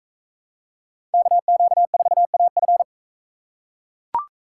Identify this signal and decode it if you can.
Morse code, heard as a keyed tone, about 42 wpm, 700 Hz. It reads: KQ4AF